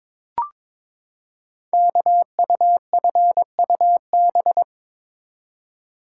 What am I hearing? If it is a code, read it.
Morse code, heard as a keyed tone, about 22 wpm, 700 Hz. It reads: KUFUB
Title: Code Groups - 5 Letters - 22wpm